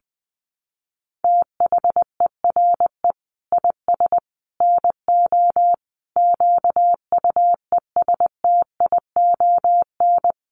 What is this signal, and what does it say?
Morse code, heard as a keyed tone, about 20 wpm, 700 Hz. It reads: THERE IS NO QUESTION
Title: Sets of 5 Words Encore 20wpm